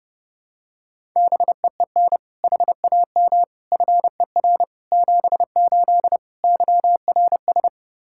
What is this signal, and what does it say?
Morse code, heard as a keyed tone, about 32 wpm, 700 Hz. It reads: BEEN HAM FER 78 YRS
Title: Practice QSO - Exchange 3 - More Info - 30wpm